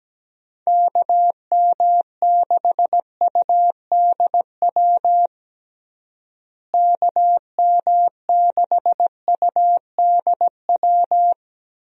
Morse code, heard as a keyed tone, about 17 wpm, 700 Hz. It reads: KM6UDW KM6UDW
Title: USA Call Signs - 6 Characters - Repeated 3x - 17wpm